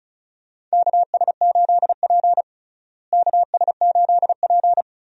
Morse code, heard as a keyed tone, about 36 wpm, 700 Hz. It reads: KS8P KS8P
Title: USA Call Signs - 4 Characters - Repeated 3x - 35wpm